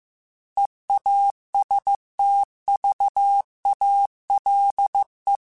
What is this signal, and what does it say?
Morse code, heard as a keyed tone, about 15 wpm, 800 Hz. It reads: EASTVALE
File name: fromSMs / FromSMs mp3s / Cities@13